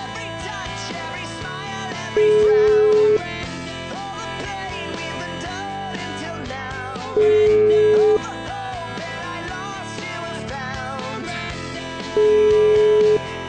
Ringback tone (Europe, 425 Hz), with noise.